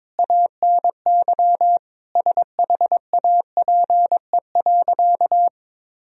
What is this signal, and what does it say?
Morse code, heard as a keyed tone, about 22 wpm, 700 Hz. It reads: ANY SHAPE.
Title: Sentences from Top 1000 Words - Review Part 3 - 22wpm